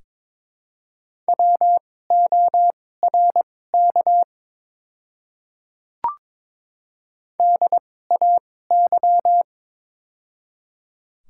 Morse code, heard as a keyed tone, about 22 wpm, spaced as for 16 wpm, 700 Hz. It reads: WORK DAY
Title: 2x Character Spacing - Top 200 Words - 22wpm